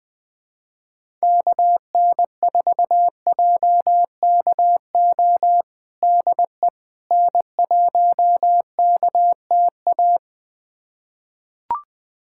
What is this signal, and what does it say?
Morse code, heard as a keyed tone, about 20 wpm, 700 Hz. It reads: KN4JKO DE N1KTA
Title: USA Callsign DE Callsign 20wpm